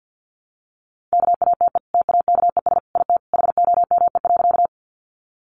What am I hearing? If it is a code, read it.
Morse code, heard as a keyed tone, about 50 wpm, 700 Hz. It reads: KATE TAKES IT HOME.
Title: Easy Phrases 50wpm